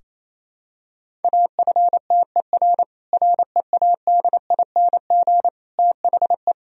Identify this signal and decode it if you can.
Morse code, heard as a keyed tone, about 28 wpm, 700 Hz. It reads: AFTER READING THE
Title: Sentences from Top 1000 Words - Review Part 3 - 28wpm